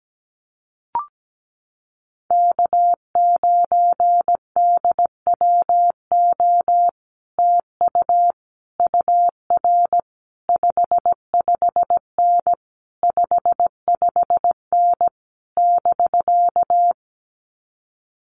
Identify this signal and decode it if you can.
Morse code, heard as a keyed tone, about 17 wpm, 700 Hz. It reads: K9DWO TU UR 55N 55N <BK>
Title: Common QSO Phrases Encore 17wpm